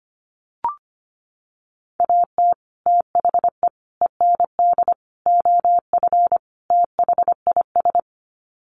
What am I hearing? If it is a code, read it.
Morse code, heard as a keyed tone, about 25 wpm, 700 Hz. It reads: AT THE END OF THIS
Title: Sets of 5 Words 25wpm